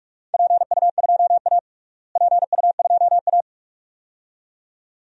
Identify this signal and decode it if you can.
Morse code, heard as a keyed tone, about 46 wpm, 700 Hz. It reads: PU2A PU2A
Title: DXCC-100 Call Signs - 4 Characters - Repeated 2x - 45wpm